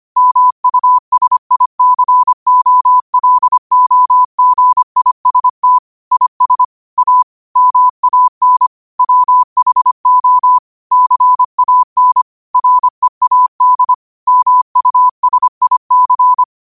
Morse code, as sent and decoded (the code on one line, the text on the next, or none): -- ..- ... .. -.-. --- .-.. --- --. .. ... - / .. ... / .- / -- .- -. / .-- .... --- / -.-. .- -. / .-. .- -.. / -- ..- ... .. -.-.
MUSICOLOGIST IS A MAN WHO CAN READ MUSIC